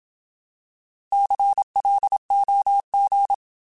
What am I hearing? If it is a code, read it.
Morse code, heard as a keyed tone, about 27 wpm, 800 Hz. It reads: CLOG
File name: Alphabetical / Alphabetical mp3s / Cs @25